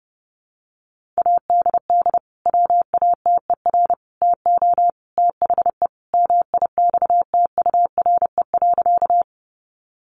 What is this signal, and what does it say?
Morse code, heard as a keyed tone, about 30 wpm, 700 Hz. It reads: ADD WATER TO THE MIXTURE.
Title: Sentences from Top 1500 Words 30wpm